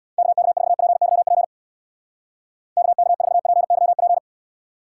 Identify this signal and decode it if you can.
Morse code, heard as a keyed tone, about 63 wpm, 700 Hz. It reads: KK4PYP KK4PYP